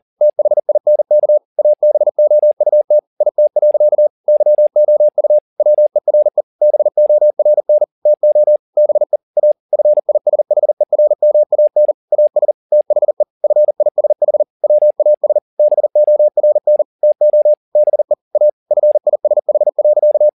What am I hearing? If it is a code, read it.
Morse code, heard as a keyed tone, about 40 wpm, 600 Hz. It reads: THINK ABOUT IT. YOU WERE BORN TO BE A FISHERMAN AS THE FISH WAS BORN TO BE A FISH.
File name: the old man and the sea 2 40wpm